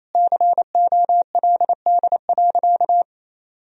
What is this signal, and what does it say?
Morse code, heard as a keyed tone, about 28 wpm, 700 Hz. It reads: COLD.
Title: Sentences from Top 1000 Words - Review Part 1 - 28wpm